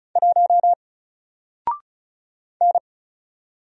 Morse code, heard as a keyed tone, about 36 wpm, 700 Hz. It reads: 1 N
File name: Single-Letter-Number-35wpm